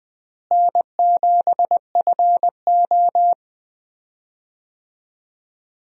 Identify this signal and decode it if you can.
Morse code, heard as a keyed tone, about 21 wpm, 700 Hz. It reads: N7FO
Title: USA Call Signs - 4 Characters - Repeated 2x - 20wpm